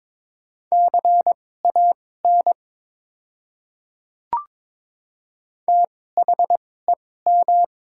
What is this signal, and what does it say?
Morse code, heard as a keyed tone, about 22 wpm, spaced as for 16 wpm, 690 Hz. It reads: CAN THEM
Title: 2x Character Spacing - Top 100 Words - 22wpm